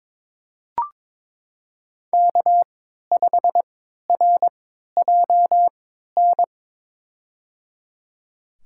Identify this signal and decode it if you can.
Morse code, heard as a keyed tone, about 22 wpm, spaced as for 12 wpm, 700 Hz. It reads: K5RJN
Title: USA Call Signs - 5 Characters - 3x Character Spacing - 22wpm